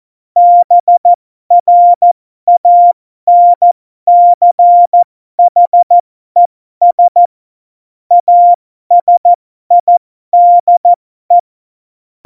Morse code, heard as a keyed tone, about 14 wpm, spaced as for 12 wpm, 700 Hz. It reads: BRANCHES ASIDE
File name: Morsekurs / Witze